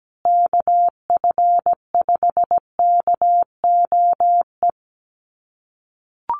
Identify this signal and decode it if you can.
Morse code, heard as a keyed tone, about 17 wpm, 700 Hz. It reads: KF5KOE
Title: USA Call Signs Repeated 3x 17wpm